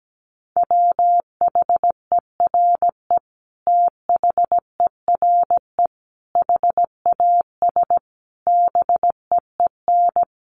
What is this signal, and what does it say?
Morse code, heard as a keyed tone, about 17 wpm, 700 Hz. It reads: WHERE THERE HAS BEEN
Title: Sentences from Top 1400 Words 17wpm